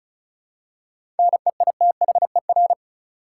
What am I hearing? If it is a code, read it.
Morse code, heard as a keyed tone, about 36 wpm, 700 Hz. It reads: NEITHER
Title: Seven Letter Words - 35wpm